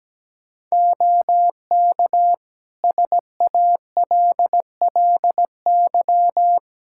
Morse code, heard as a keyed tone, about 17 wpm, 700 Hz. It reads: OK SALLY
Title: Practice QSO - Exchange 2 - Info - 17wpm